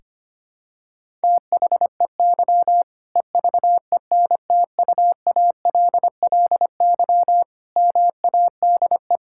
Morse code, heard as a keyed tone, about 25 wpm, 700 Hz. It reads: THEY EVENTUALLY MADE
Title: Sentences from Top 1000 Words - Review Part 6 - 25wpm